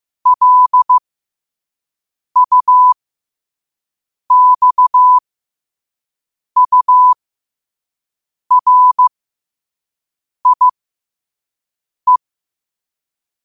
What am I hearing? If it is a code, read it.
Morse code, heard as a keyed tone, about 15 wpm, spaced as for 5 wpm, 1000 Hz. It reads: LUXURIE